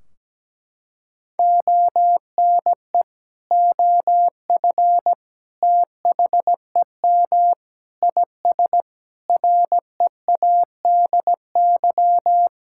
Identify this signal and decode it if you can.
Morse code, heard as a keyed tone, about 17 wpm, 700 Hz. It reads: ONE OF THEM IS READY
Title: Sentences from Top 1600 Words 17wpm